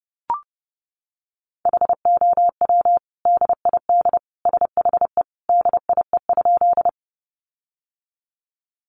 Morse code, heard as a keyed tone, about 30 wpm, 700 Hz. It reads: HOW DID SHE DIE?
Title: Sentences from Top 1300 Words 30wpm